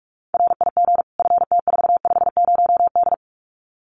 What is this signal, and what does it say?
Morse code, heard as a keyed tone, about 44 wpm, 700 Hz. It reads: RIG FT450D
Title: Common QSO Phrases 45wpm